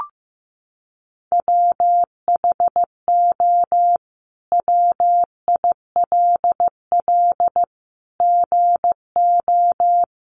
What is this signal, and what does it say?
Morse code, heard as a keyed tone, about 15 wpm, 700 Hz. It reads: WHO WILL GO
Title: Sentences from Top 100 Words 15wpm